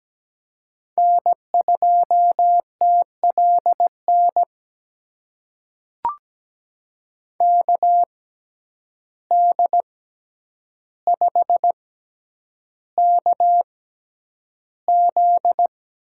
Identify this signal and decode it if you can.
Morse code, heard as a keyed tone, about 17 wpm, spaced as for 9 wpm, 700 Hz. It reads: N2TLN KD5KZ
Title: USA Call Signs - 6x Character Spacing - 17wpm